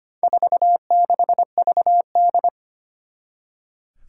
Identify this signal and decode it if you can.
Morse code, heard as a keyed tone, about 25 wpm, 700 Hz. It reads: 46VD
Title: Four Challenging Letter-Number Combinations 25wpm